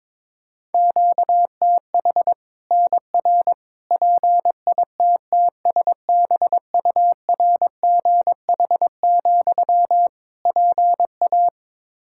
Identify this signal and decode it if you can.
Morse code, heard as a keyed tone, about 22 wpm, 700 Hz. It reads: QTH NR PITTSBURGH, PA
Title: Practice QSO - Exchange 1 - Beginning - 22wpm